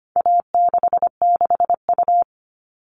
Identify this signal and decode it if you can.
Morse code, heard as a keyed tone, about 25 wpm, 700 Hz. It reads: A66U
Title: All Cut Numbers - 4-Digit 25wpm